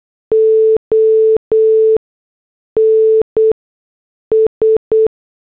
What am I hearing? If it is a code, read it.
Morse code, heard as a keyed tone, about 8 wpm, spaced as for 6 wpm, 440 Hz. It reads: ONS